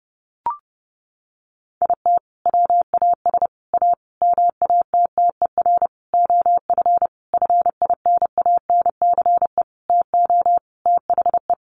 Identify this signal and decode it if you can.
Morse code, heard as a keyed tone, about 28 wpm, 700 Hz. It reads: IT WAS A MATTER OF FINANCE TO THE
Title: Sentences from Top 1500 Words 30wpm